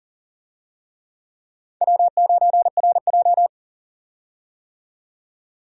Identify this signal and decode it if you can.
Morse code, heard as a keyed tone, about 40 wpm, 700 Hz. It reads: W9RJ
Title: USA Call Signs - 4 Characters - 40wpm